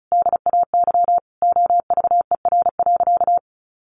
Morse code, heard as a keyed tone, about 34 wpm, 700 Hz. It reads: DAY OVER.
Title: A Sound Of Thunder 35wpm